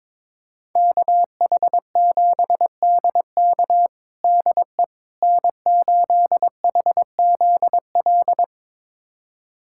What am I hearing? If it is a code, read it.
Morse code, heard as a keyed tone, about 22 wpm, 700 Hz. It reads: KH7DK DE N8HZL
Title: USA Callsign DE Callsign - 5 Characters - 22wpm